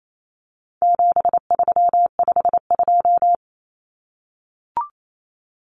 Morse code, heard as a keyed tone, about 28 wpm, 700 Hz. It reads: 7352
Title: Numbers - 4-Digit 28wpm